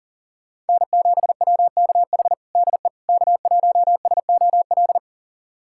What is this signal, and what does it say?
Morse code, heard as a keyed tone, about 40 wpm, 700 Hz. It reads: N7WKH DE K1SOL
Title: USA Callsign DE Callsign - 5 Characters - 40wpm